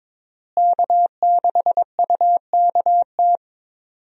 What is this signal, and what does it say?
Morse code, heard as a keyed tone, about 22 wpm, 700 Hz. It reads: K6UKT
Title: USA Call Signs - 5 Characters - Repeated 3x - 22wpm